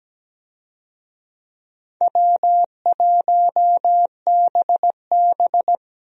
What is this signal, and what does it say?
Morse code, heard as a keyed tone, about 17 wpm, 700 Hz. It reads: W1BB
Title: USA Call Signs - 4 Characters - 17wpm